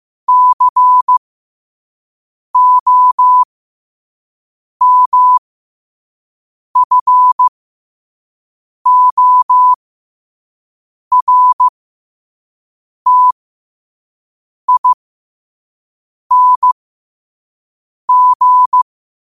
Morse code, heard as a keyed tone, about 15 wpm, spaced as for 5 wpm, 1000 Hz. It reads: COMFORTING
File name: qotdfiles / 5wpm